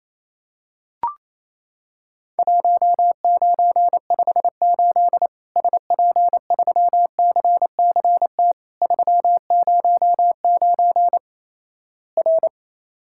Morse code, heard as a keyed tone, about 28 wpm, 700 Hz. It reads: 1958 SP3CCT 309 R
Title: Worked All Europe DX Contest - QTC Exchange - 28wpm